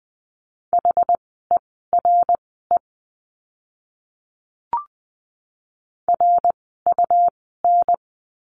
Morse code, heard as a keyed tone, about 20 wpm, spaced as for 15 wpm, 700 Hz. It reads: HERE RUN
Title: 115 Lesson 19f - Letter w - Words 20wpm